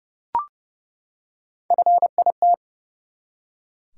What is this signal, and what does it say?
Morse code, heard as a keyed tone, about 28 wpm, 700 Hz. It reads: FIT
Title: Three Letter Words - 30wpm